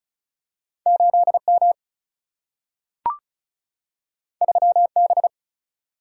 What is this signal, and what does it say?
Morse code, heard as a keyed tone, about 36 wpm, 700 Hz. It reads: 8M 3B
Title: Single-Letter Single-Number 35wpm